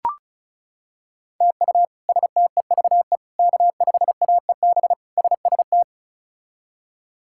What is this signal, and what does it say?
Morse code, heard as a keyed tone, about 36 wpm, 700 Hz. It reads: TU STEVE K5AEB SST
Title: K1USN SST QSO Practice - 35wpm